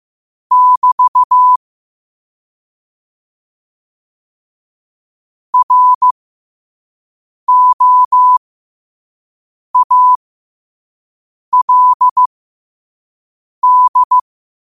Morse code, heard as a keyed tone, about 15 wpm, spaced as for 5 wpm, 1000 Hz. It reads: <BT> ROALD
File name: 2024-10-07-5wpm